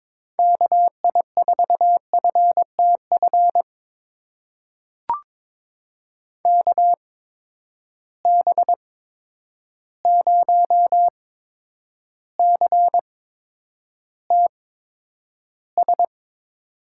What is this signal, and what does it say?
Morse code, heard as a keyed tone, about 22 wpm, spaced as for 9 wpm, 700 Hz. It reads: KI4FTF KB0CTS
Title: USA Call Signs - 6 Characters - 8x Character Spacing - 22wpm